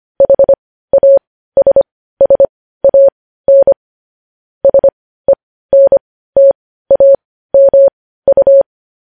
Morse code, heard as a keyed tone, about 25 wpm, spaced as for 15 wpm, 550 Hz. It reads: HASSAN SENTAMU
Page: news headlines in morse code at 15 wpm